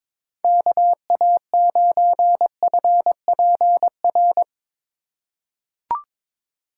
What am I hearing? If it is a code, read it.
Morse code, heard as a keyed tone, about 22 wpm, 700 Hz. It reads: KA9FPR
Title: USA Call Signs - 6 Characters - 2x Character Spacing - 22wpm